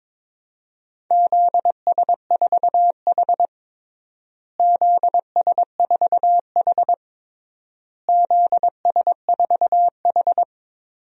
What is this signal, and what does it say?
Morse code, heard as a keyed tone, about 22 wpm, 700 Hz. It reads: ZS4H ZS4H ZS4H
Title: DXCC-100 Call Signs - 4 Characters - Repeated 3x - 22wpm